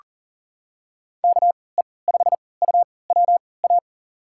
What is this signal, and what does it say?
Morse code, heard as a keyed tone, about 38 wpm, spaced as for 22 wpm, 700 Hz. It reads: KE5UWA
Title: USA Call Signs - 6 Characters - 3x Character Spacing - 40wpm